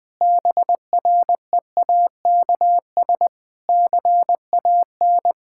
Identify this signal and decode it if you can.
Morse code, heard as a keyed tone, about 20 wpm, 700 Hz. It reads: BREAKS CAN
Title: Difficult Sentences with Hints - 20wpm